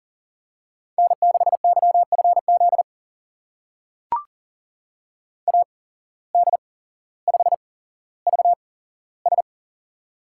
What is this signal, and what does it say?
Morse code, heard as a keyed tone, about 41 wpm, spaced as for 17 wpm, 700 Hz. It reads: N6YFZ AD5VS
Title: USA Call Signs - 8x Character Spacing - 40wpm